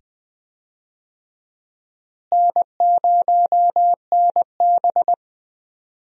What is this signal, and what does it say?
Morse code, heard as a keyed tone, about 20 wpm, 700 Hz. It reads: N0NB
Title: USA Call Signs - 4 Characters - Repeated 2x - 20wpm